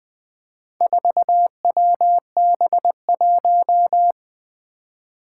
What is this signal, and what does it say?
Morse code, heard as a keyed tone, about 20 wpm, 700 Hz. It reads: 4WB1
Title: Four Challenging Letter-Number Combinations 20wpm